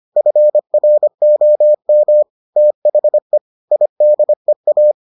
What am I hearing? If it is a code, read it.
Morse code, heard as a keyed tone, about 25 wpm, 600 Hz. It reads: FROM THE IDEA